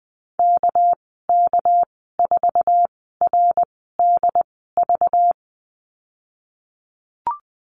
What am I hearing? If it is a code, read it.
Morse code, heard as a keyed tone, about 20 wpm, spaced as for 15 wpm, 700 Hz. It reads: KK4RDV
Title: 213 Lesson 34g - Number 8 - Call Signs 20wpm